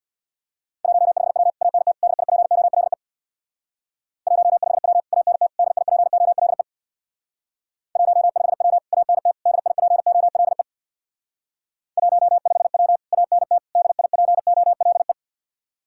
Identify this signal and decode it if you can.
Morse code, heard as a keyed tone, about 64 wpm, 700 Hz. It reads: 15W ANT DIPOLE 15W ANT DIPOLE 15W ANT DIPOLE 15W ANT DIPOLE